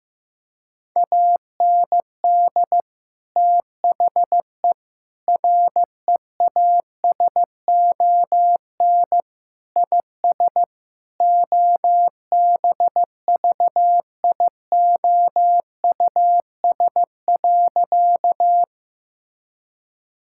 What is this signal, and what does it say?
Morse code, heard as a keyed tone, about 15 wpm, 700 Hz. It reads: AND THE REASON IS OBVIOUS.
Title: Sentences from Top 1600 Words 15wpm